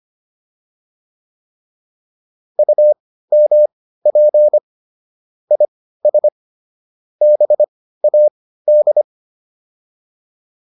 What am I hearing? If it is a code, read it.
Morse code, heard as a keyed tone, about 25 wpm, spaced as for 15 wpm, 600 Hz. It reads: UMP IS BAD